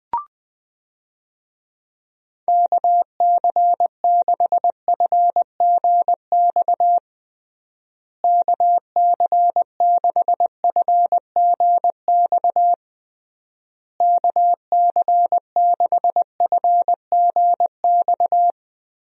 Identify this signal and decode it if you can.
Morse code, heard as a keyed tone, about 20 wpm, 700 Hz. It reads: KC6FGX KC6FGX KC6FGX
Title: USA Call Signs - 6 Characters - Repeated 3x - 20wpm